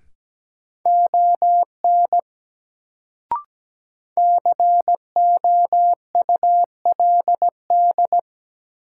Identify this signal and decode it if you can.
Morse code, heard as a keyed tone, about 17 wpm, 700 Hz. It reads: ON COULD
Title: Sentences from Top 100 Words - New Word Drill - 17wpm